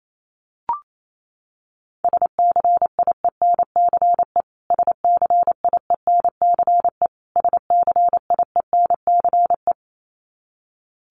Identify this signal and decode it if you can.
Morse code, heard as a keyed tone, about 28 wpm, 700 Hz. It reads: SCIENCE SCIENCE SCIENCE
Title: Top 500 Words Repeated 28wpm